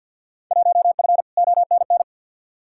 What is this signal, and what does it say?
Morse code, heard as a keyed tone, about 47 wpm, 700 Hz. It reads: 1F ONN
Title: Field Day Exchanges Encore - 50wpm